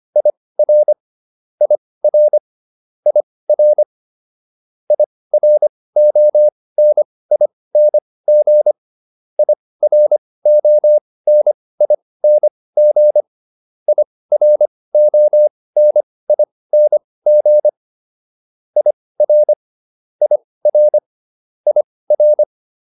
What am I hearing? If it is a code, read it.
Morse code, heard as a keyed tone, about 25 wpm, spaced as for 18 wpm, 600 Hz. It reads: IR IR IR IRONING IRONING IRONING IR IR IR